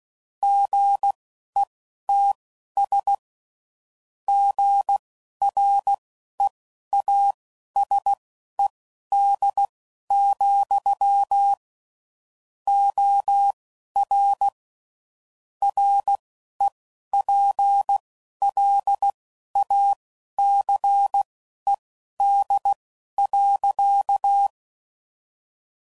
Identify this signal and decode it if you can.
Morse code, heard as a keyed tone, about 16 wpm, spaced as for 11 wpm, 800 Hz. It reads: GETS GREASED, OR REPLACED.
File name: Sayings / Chuckles / Squeaky